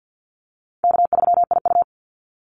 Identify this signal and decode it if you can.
Morse code, heard as a keyed tone, about 50 wpm, 700 Hz. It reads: K3IU
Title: Easy Call Signs - 50wpm